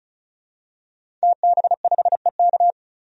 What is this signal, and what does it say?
Morse code, heard as a keyed tone, about 36 wpm, 700 Hz. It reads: TB5EK
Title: DXCC-100 Call Signs - 5 Characters - Repeated 3x - 35wpm